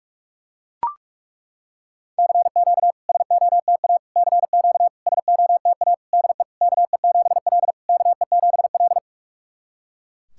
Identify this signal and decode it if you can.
Morse code, heard as a keyed tone, about 41 wpm, 690 Hz. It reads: CQ SOTA CQ SOTA DE KE7L KE7L
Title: SOTA QSO Practice 45wpm